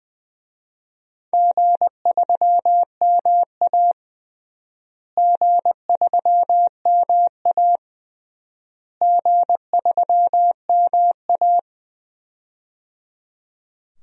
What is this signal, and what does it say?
Morse code, heard as a keyed tone, about 20 wpm, 700 Hz. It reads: G3MA G3MA G3MA
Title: DXCC-100 Call Signs - 4 Characters - Repeated 3x - 20wpm